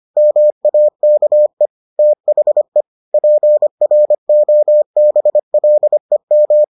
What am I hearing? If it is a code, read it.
Morse code, heard as a keyed tone, about 25 wpm, 600 Hz. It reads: MAKE THE PROBLEM